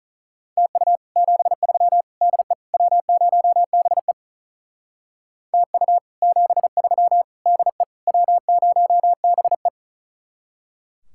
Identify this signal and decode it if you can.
Morse code, heard as a keyed tone, about 44 wpm, 700 Hz. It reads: TU 73 DE W0BE TU 73 DE W0BE